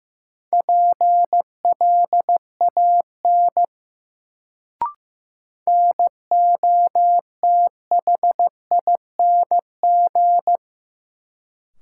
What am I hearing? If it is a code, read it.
Morse code, heard as a keyed tone, about 15 wpm, 700 Hz. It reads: PLAN NOTHING
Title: Sentences from Top 500 Words - New Word Drill - 15wpm